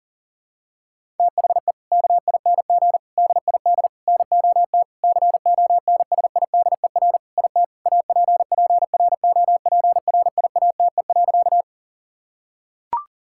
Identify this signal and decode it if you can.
Morse code, heard as a keyed tone, about 37 wpm, 700 Hz. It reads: THE KING DID NOT CONSIDER IT APPROPRIATE.
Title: Sentences from Top 1000 Words - Review Part 5 - 40wpm